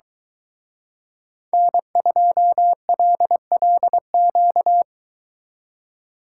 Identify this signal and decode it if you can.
Morse code, heard as a keyed tone, about 23 wpm, 700 Hz. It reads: N2LLQ